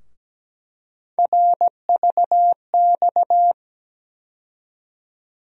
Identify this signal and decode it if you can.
Morse code, heard as a keyed tone, about 17 wpm, 700 Hz. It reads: RVX
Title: USA Call Sign Postfixes - 3 Letters 17wpm